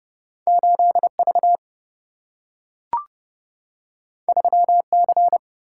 Morse code, heard as a keyed tone, about 31 wpm, 700 Hz. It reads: 8 V 3 C
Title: Single-Letter Single-Number 30wpm